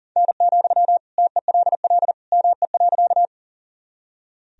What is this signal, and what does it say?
Morse code, heard as a keyed tone, about 40 wpm, 700 Hz. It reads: N, TELL ME.
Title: Sentences from Top 200 Words and Abbreviations 40wpm